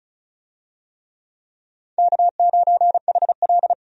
Morse code, heard as a keyed tone, about 36 wpm, 700 Hz. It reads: K9HL
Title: USA Call Signs - 4 Characters - 35wpm